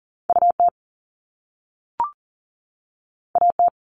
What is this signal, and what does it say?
Morse code, heard as a keyed tone, about 38 wpm, 700 Hz. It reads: UT AT
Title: Top Two Letter Combinations in Words 40wpm